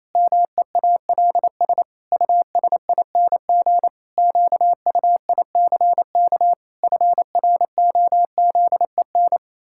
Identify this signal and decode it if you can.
Morse code, heard as a keyed tone, about 28 wpm, 700 Hz. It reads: MEALS USING QUICK FROZEN
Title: Pangram Challenge v2 - 28wpm